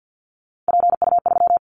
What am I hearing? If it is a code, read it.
Morse code, heard as a keyed tone, about 50 wpm, 700 Hz. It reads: PU3
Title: Callsign Prefixes for Top 50 DXCC - 50wpm